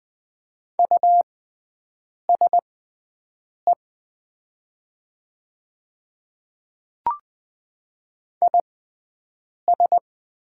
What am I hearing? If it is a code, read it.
Morse code, heard as a keyed tone, about 20 wpm, spaced as for 7 wpm, 700 Hz. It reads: USE IS
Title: 6x Character Spacing - Top 500 Words - 20wpm